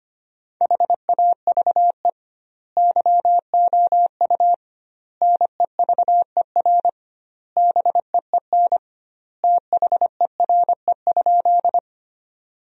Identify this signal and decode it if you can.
Morse code, heard as a keyed tone, about 25 wpm, 700 Hz. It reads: HAVE YOU NEVER BEEN THERE?
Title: 2x Word-Spacing - Sentences from Top 300 Words - 25wpm